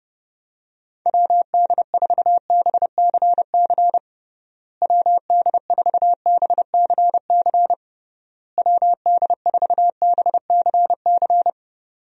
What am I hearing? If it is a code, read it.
Morse code, heard as a keyed tone, about 30 wpm, 700 Hz. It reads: WD4BCC WD4BCC WD4BCC
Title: USA Call Signs - 6 Characters - Repeated 3x - 30wpm